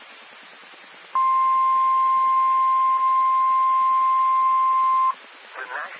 Emergency Alert System (SAME) tones (the 1050 Hz attention tone). It is heard from 1.1 to 5.2 s.
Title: Nassau and Suffolk County Amateur Repeaters